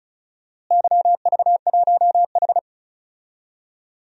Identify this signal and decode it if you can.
Morse code, heard as a keyed tone, about 36 wpm, 700 Hz. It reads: YV1H
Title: DXCC-100 Call Signs - 4 Characters - Repeated 2x - 35wpm